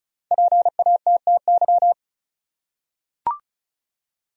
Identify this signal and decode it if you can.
Morse code, heard as a keyed tone, about 36 wpm, 700 Hz. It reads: PATTY
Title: USA Common Names - Max 5 Letters - 35wpm